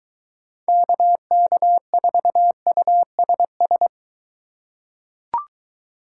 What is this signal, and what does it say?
Morse code, heard as a keyed tone, about 21 wpm, 700 Hz. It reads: KK4USS